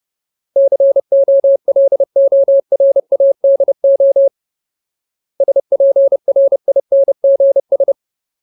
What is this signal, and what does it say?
Morse code, heard as a keyed tone, about 30 wpm, 550 Hz. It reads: COLORADO SPRINGS
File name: Ultimate Word List / 30WPM Ultimate